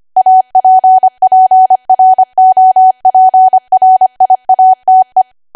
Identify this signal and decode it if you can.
Morse code, heard as a keyed tone, about 25 wpm, 750 Hz. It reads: APPROPRIATE
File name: Archive / 25 WPM / 25WPM